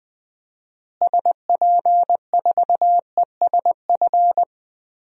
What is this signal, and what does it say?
Morse code, heard as a keyed tone, about 20 wpm, 700 Hz. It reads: SP4ESF
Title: DXCC-100 Call Signs - 6 Characters - Repeated 3x - 20wpm